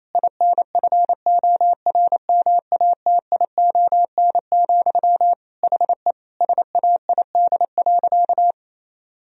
Morse code, heard as a keyed tone, about 27 wpm, 690 Hz. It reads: INFORMATION, HE SAID.
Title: Sentences from Top 1000 Words - Review Part 3 - 28wpm